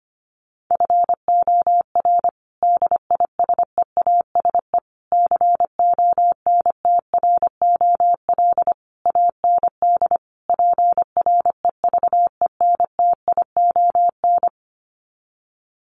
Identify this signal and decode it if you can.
Morse code, heard as a keyed tone, about 25 wpm, 700 Hz. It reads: FOR DISEASE CONTROL AND PREVENTION
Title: Sets of 5 Words Encore2 25wpm